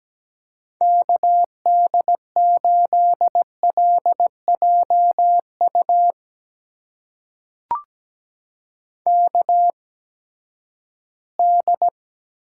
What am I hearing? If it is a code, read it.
Morse code, heard as a keyed tone, about 17 wpm, spaced as for 12 wpm, 700 Hz. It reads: KD8LJU KD
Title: USA Call Signs - 6 Characters - 8x Character Spacing - 17wpm